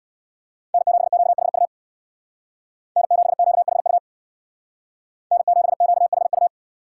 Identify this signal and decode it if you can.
Morse code, heard as a keyed tone, about 66 wpm, 700 Hz. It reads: N7QHU N7QHU N7QHU